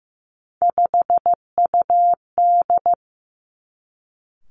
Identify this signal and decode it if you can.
Morse code, heard as a keyed tone, about 15 wpm, 700 Hz. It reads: 5UD
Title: Three Challenging Letter-Number Combinations 15wpm